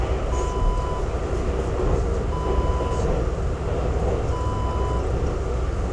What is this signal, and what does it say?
There is no code.